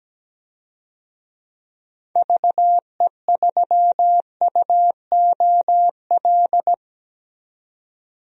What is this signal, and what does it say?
Morse code, heard as a keyed tone, about 17 wpm, 700 Hz. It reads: VE3UOL
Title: DXCC-100 Call Signs - 6 Characters - 17wpm